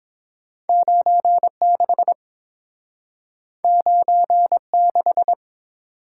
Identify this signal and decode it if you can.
Morse code, heard as a keyed tone, about 30 wpm, 700 Hz. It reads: 96 96